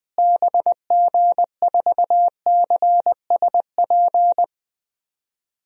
Morse code, heard as a keyed tone, about 21 wpm, 700 Hz. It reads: BG4CSP
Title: DXCC-100 Call Signs - 6 Characters - Repeated 3x - 20wpm